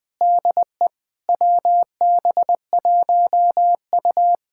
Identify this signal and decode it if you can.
Morse code, heard as a keyed tone, about 20 wpm, 700 Hz. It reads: DE WB1U
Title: Speed-Racing Common QSO Phrases - 20wpm